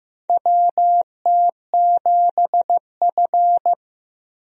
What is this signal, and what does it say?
Morse code, heard as a keyed tone, about 15 wpm, 700 Hz. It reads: WT7F
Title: USA Call Signs - 4 Characters - Repeated 3x - 15wpm